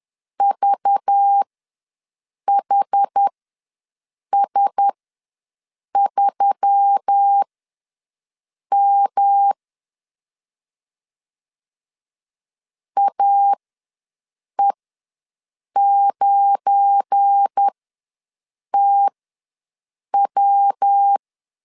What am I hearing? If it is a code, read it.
Morse code, heard as a keyed tone, about 10 wpm, spaced as for 6 wpm, 800 Hz. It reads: VHS3M AE9TW
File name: LEZIONI / 7 GF9Z / 7 GF9 ws 10wpm 4034